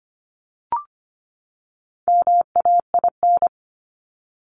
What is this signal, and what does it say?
Morse code, heard as a keyed tone, about 25 wpm, 700 Hz. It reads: MAIN
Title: Top 300 Words 25wpm